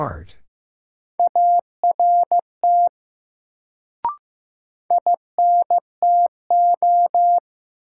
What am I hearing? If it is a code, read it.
Morse code, heard as a keyed tone, about 15 wpm, 700 Hz. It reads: ART INTO
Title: Sentences from Top 600 Words - New Word Drill - 15wpm